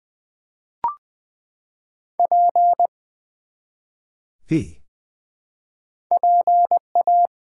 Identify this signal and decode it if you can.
Morse code, heard as a keyed tone, about 20 wpm, 700 Hz. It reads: P PA
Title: Build A Word - 1 Letter At A Time - 3-Letter Words - 20wpm